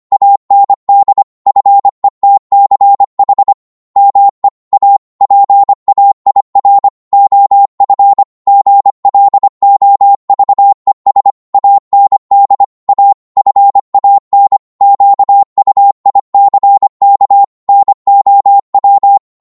Morse code, heard as a keyed tone, about 25 wpm, 800 Hz. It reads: AND FETCH ME A PAIR OF GLOVES AND A FAN QUICK NOW
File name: 25wpm